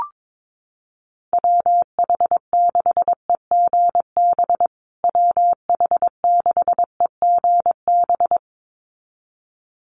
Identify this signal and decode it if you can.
Morse code, heard as a keyed tone, about 22 wpm, 700 Hz. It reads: WH6EGB WH6EGB
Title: USA Call Signs Repeated 2x 22wpm